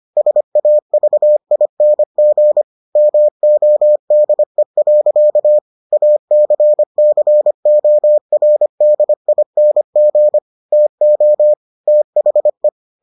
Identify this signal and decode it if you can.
Morse code, heard as a keyed tone, about 25 wpm, 600 Hz. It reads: SAVING MODE. ACCORDING TO THE